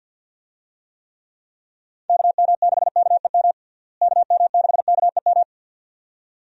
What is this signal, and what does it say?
Morse code, heard as a keyed tone, about 53 wpm, 700 Hz. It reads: KM6KEM KM6KEM
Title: USA Call Signs - 6 Characters - Repeated 2x - 50wpm